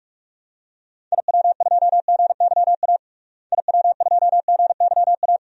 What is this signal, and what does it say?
Morse code, heard as a keyed tone, about 46 wpm, 700 Hz. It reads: IW2GYA IW2GYA
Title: DXCC-100 Call Signs - 6 Characters - Repeated 2x - 45wpm